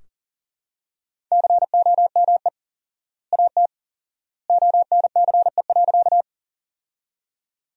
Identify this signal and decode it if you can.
Morse code, heard as a keyed tone, about 37 wpm, 700 Hz. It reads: COME AT ONCE.
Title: 4x Word-Spacing - Sentences from Top 300 Words - 40wpm